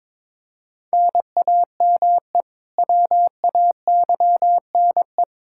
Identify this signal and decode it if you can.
Morse code, heard as a keyed tone, about 22 wpm, 700 Hz. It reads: NAME WAYNE
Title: Practice QSO - Exchange 1 - Beginning - 22wpm